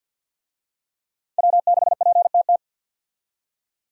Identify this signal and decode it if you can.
Morse code, heard as a keyed tone, about 51 wpm, 700 Hz. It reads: W6PTT
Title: USA Call Signs - 5 Characters - 50wpm